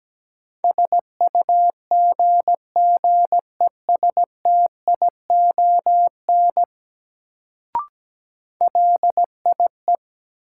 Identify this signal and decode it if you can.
Morse code, heard as a keyed tone, about 17 wpm, 700 Hz. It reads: SUGGESTION LIE
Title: Sentences from Top 1600 Words - New Word Drill - 17wpm